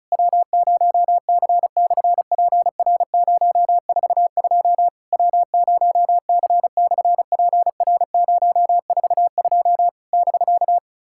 Morse code, heard as a keyed tone, about 33 wpm, 700 Hz. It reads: W0C/PR042 W0C/PR042 <BK>
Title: SOTA QSO Practice 35wpm